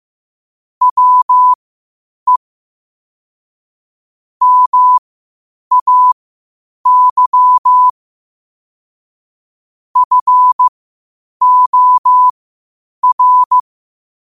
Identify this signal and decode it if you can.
Morse code, heard as a keyed tone, about 15 wpm, spaced as for 8 wpm, 1000 Hz. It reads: WE MAY FOR
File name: qotdfiles / 8wpm